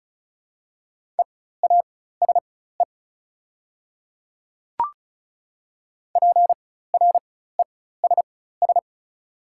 Morse code, heard as a keyed tone, about 36 wpm, spaced as for 16 wpm, 700 Hz. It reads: EASE PRESS
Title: 4x Character Spacing - Top 500 Words - 35wpm